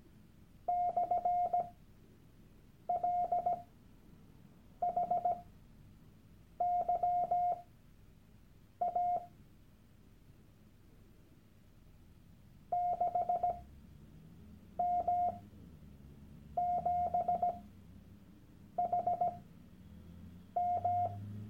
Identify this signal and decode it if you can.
Morse code, heard as a keyed tone, about 17 wpm, spaced as for 6 wpm, 710 Hz. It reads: /LHYA 6M7HM
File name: Workshop Dateien / CW-Online-Kurs / Lektion 001